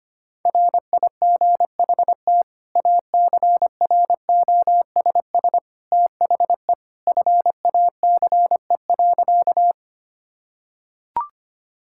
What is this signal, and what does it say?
Morse code, heard as a keyed tone, about 25 wpm, 700 Hz. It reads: RIGHT ACROSS THE FACE.
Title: Sentences from Top 1000 Words - Review Part 5 - 25wpm